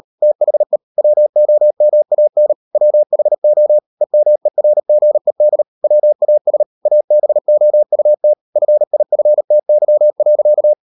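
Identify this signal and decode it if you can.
Morse code, heard as a keyed tone, about 38 wpm, 600 Hz. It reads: THE WOMAN WHO EMERGED WAS ABOUT FIFTY.
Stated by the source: Morse code practice at 38 WPM, sending Ken Follett's novel Never in English, from CW med Gnister.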